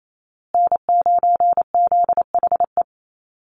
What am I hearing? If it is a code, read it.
Morse code, heard as a keyed tone, about 28 wpm, 700 Hz. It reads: N9ZHE
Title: USA Call Signs Repeated 2x 28wpm